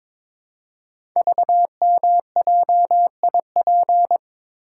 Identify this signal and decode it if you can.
Morse code, heard as a keyed tone, about 22 wpm, 700 Hz. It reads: VMJIP
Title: Code Groups - 5 Letter:Numbers - 22wpm